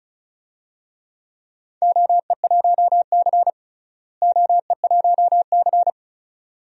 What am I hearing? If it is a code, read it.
Morse code, heard as a keyed tone, about 36 wpm, 700 Hz. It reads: OE1C OE1C
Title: DXCC-100 Call Signs - 4 Characters - Repeated 2x - 35wpm